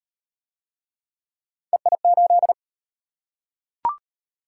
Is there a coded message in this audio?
Yes: Morse code from 1.7 to 2.5 s, heard as a keyed tone, about 36 wpm, 700 Hz. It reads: EI8